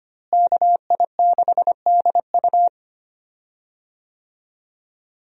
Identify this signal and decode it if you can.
Morse code, heard as a keyed tone, about 25 wpm, 700 Hz. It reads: KI6DU
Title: USA Call Signs - 5 Characters - Repeated 2x - 25wpm